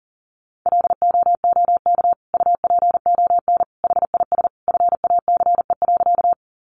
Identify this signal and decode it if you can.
Morse code, heard as a keyed tone, about 40 wpm, 700 Hz. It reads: LOOK UPON HIS FACE.
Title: Sentences from Top 1100 Words 40wpm